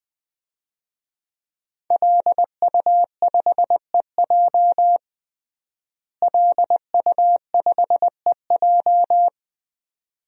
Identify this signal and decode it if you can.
Morse code, heard as a keyed tone, about 20 wpm, 700 Hz. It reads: LU5EJ LU5EJ
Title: DXCC-100 Call Signs - 5 Characters - Repeated 2x - 20wpm